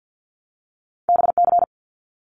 Morse code, heard as a keyed tone, about 50 wpm, 700 Hz. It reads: BC
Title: Canadian Provinces and Territories 50wpm